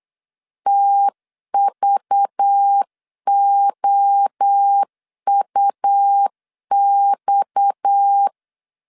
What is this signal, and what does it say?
Morse code, heard as a keyed tone, about 8 wpm, 800 Hz. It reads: TVOUX
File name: CW / DET / 1 6 1500C S C 8wpm 5219